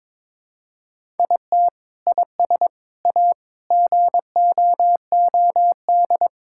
Morse code, heard as a keyed tone, about 21 wpm, 700 Hz. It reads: IT IS A GOOD
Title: Sentences from Top 1000 Words - Review Part 5 - 22wpm